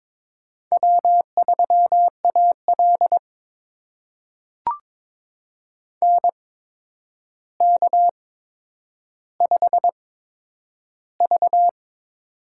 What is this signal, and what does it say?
Morse code, heard as a keyed tone, about 22 wpm, spaced as for 9 wpm, 700 Hz. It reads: W3AL NK5V
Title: USA Call Signs - 4 Characters - 8x Character Spacing - 22wpm